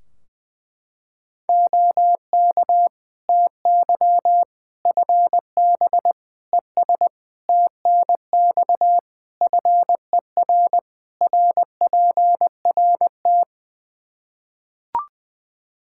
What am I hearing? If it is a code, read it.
Morse code, heard as a keyed tone, about 20 wpm, 700 Hz. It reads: OK TY FB ES TNX FER RPRT
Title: Practice QSO - Exchange 2 - Info - 20wpm